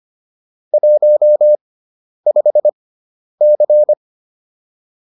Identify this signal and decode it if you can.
Morse code, heard as a keyed tone, about 25 wpm, spaced as for 10 wpm, 600 Hz. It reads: 15C